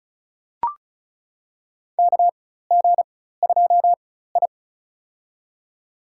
Morse code, heard as a keyed tone, about 36 wpm, spaced as for 16 wpm, 700 Hz. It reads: KG2I
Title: USA Call Signs - 4 Characters - 4x Character Spacing - 35wpm